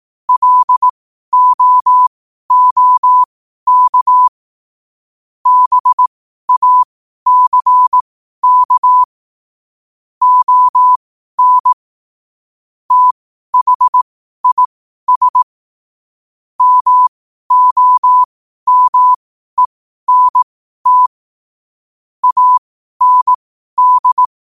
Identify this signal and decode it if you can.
Morse code, heard as a keyed tone, about 18 wpm, spaced as for 12 wpm, 1000 Hz. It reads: LOOK BACK ON THIS MOMENT AND
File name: qotdfiles / 12wpm